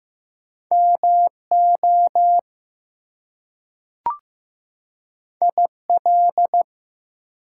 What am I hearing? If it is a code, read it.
Morse code, heard as a keyed tone, about 15 wpm, 700 Hz. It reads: MO IL
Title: US States and Canadian Provinces 15wpm